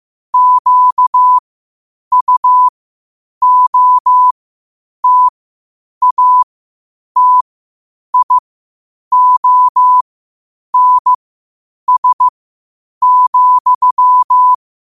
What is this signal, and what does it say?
Morse code, heard as a keyed tone, about 15 wpm, spaced as for 8 wpm, 1000 Hz. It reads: QUOTATIONS,
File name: qotdfiles / 8wpm